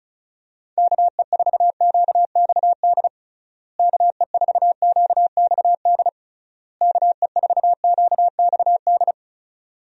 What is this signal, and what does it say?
Morse code, heard as a keyed tone, about 36 wpm, 700 Hz. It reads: KE4QXD KE4QXD KE4QXD
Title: USA Call Signs - 6 Characters - Repeated 3x - 35wpm